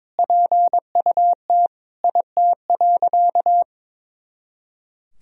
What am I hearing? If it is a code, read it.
Morse code, heard as a keyed tone, about 22 wpm, 700 Hz. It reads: PUT IT.
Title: Sentences from Top 200 Words and Abbreviations 22wpm